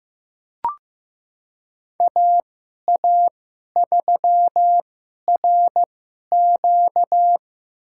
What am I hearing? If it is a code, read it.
Morse code, heard as a keyed tone, about 15 wpm, spaced as for 11 wpm, 700 Hz. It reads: AA3RQ
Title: USA Call Signs - 2x Character Spacing - 15wpm